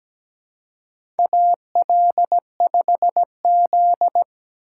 Morse code, heard as a keyed tone, about 17 wpm, 700 Hz. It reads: AL5Z
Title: USA Call Signs - 4 Characters - 2x Character Spacing - 17wpm